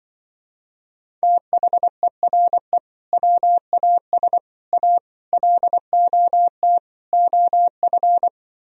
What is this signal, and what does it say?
Morse code, heard as a keyed tone, about 24 wpm, 700 Hz. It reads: THERE WAS A LOT OF
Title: Speed-Racing - Sets of 5 Words 20wpm